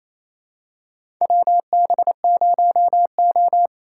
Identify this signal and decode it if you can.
Morse code, heard as a keyed tone, about 28 wpm, 700 Hz. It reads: WB0O
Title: USA Call Signs - 4 Characters - 3x Character Spacing - 28wpm